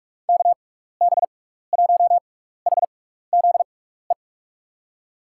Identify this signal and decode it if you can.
Morse code, heard as a keyed tone, about 44 wpm, spaced as for 15 wpm, 700 Hz. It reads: KB1HZE
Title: USA Call Signs - 6 Characters - 6x Character Spacing - 45wpm